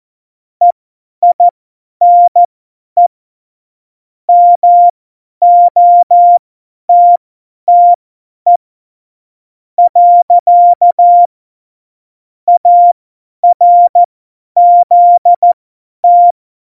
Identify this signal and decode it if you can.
Morse code, heard as a keyed tone, about 14 wpm, spaced as for 10 wpm, 700 Hz. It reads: EINE MOTTE . ARZT